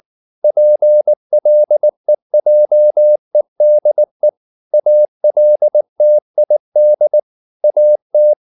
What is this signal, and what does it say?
Morse code, heard as a keyed tone, about 19 wpm, 600 Hz. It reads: PLEJEDE ALTID AT
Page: Ferie 19 wpm